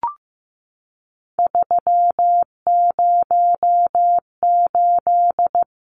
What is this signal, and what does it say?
Morse code, heard as a keyed tone, about 15 wpm, 700 Hz. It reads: 308
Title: Numbers - 3-Digit 15wpm